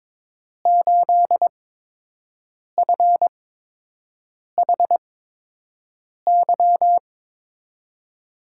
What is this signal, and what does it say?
Morse code, heard as a keyed tone, about 21 wpm, spaced as for 6 wpm, 700 Hz. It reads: 8FHY
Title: USA Call Signs - 8x Character Spacing - 22wpm